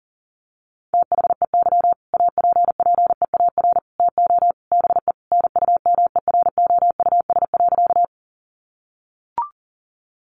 Morse code, heard as a keyed tone, about 40 wpm, 700 Hz. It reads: THEY APPEAR TO BE NUMEROUS.
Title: Sentences from Top 1400 Words 40wpm